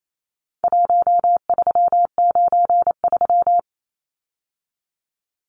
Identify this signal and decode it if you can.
Morse code, heard as a keyed tone, about 28 wpm, 700 Hz. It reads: 1393
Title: Numbers - 4-Digit 28wpm